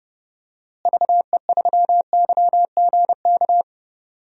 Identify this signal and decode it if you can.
Morse code, heard as a keyed tone, about 30 wpm, 700 Hz. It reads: VE3YGK
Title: DXCC-100 Call Signs - 6 Characters - Repeated 3x - 30wpm